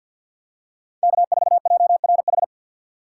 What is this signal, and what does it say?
Morse code, heard as a keyed tone, about 51 wpm, 700 Hz. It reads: K4JRH
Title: USA Call Signs - 4x Character Spacing - 50wpm